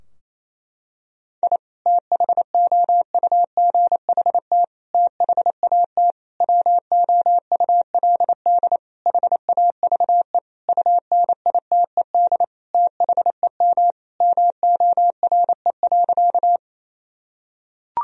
Morse code, heard as a keyed tone, about 27 wpm, 700 Hz. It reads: I THOUGHT THAT WOULD HAVE UNITED THEM MORE.
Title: Sentences from Top 1000 Words - Review Part 4 - 28wpm